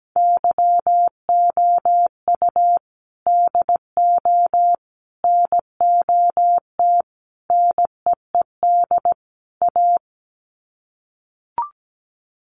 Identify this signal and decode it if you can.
Morse code, heard as a keyed tone, about 17 wpm, 700 Hz. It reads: YOU DO NOT NEED A
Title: Sets of 5 Words Encore 17wpm